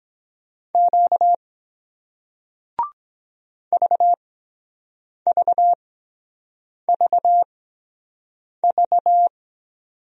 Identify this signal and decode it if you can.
Morse code, heard as a keyed tone, about 23 wpm, spaced as for 7 wpm, 700 Hz. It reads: Q VVVV